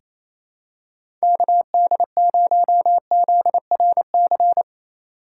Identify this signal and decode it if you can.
Morse code, heard as a keyed tone, about 28 wpm, 700 Hz. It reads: KD0ZRC
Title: USA Call Signs - 6 Characters - 3x Character Spacing - 28wpm